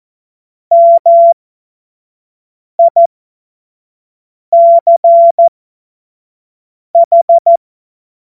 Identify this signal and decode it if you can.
Morse code, heard as a keyed tone, about 14 wpm, spaced as for 5 wpm, 680 Hz. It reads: MICH
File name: Morsekurs / Witze / Witz 22. Juni 2021, 25 F70